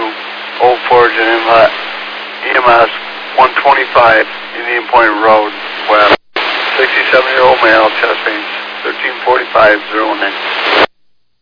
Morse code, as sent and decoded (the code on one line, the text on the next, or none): none